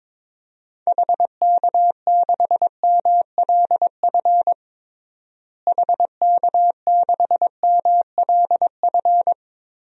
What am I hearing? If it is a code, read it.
Morse code, heard as a keyed tone, about 22 wpm, 700 Hz. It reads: HK6MLF HK6MLF
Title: DXCC-100 Call Signs - 6 Characters - Repeated 2x - 22wpm